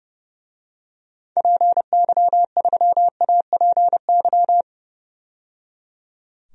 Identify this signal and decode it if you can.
Morse code, heard as a keyed tone, about 30 wpm, 700 Hz. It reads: PY3APY
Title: DXCC-100 Call Signs - 6 Characters - 30wpm